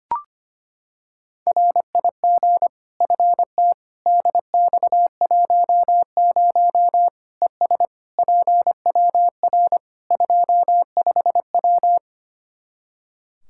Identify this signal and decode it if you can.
Morse code, heard as a keyed tone, about 25 wpm, 700 Hz. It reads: RIG FT DX10 ES PWR 25W
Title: Practice QSO - Exchange 2 - Info - 25wpm